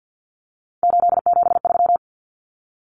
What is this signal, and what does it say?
Morse code, heard as a keyed tone, about 50 wpm, 700 Hz. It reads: 873
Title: Numbers - 3-Digit 50wpm